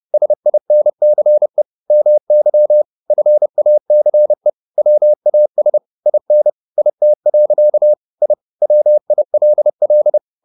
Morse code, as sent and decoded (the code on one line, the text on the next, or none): ... .. -. -.-. / -- -.-- / ..-. .- -.-. / .-- .- ... / .. -. / .. - .-.-.- / .. / .-- .. .-.. .-..
SINCE MY FACE WAS IN IT. I WILL